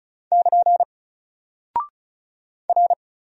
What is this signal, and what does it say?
Morse code, heard as a keyed tone, about 36 wpm, 700 Hz. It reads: <KN> R
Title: Single Character (Letter:Number:Punctuation:Prosign) Alt 35wpm